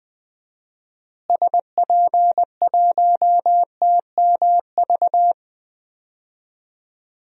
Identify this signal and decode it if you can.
Morse code, heard as a keyed tone, about 21 wpm, 700 Hz. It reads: SP1TMV
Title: DXCC-100 Call Signs - 6 Characters - Repeated 2x - 20wpm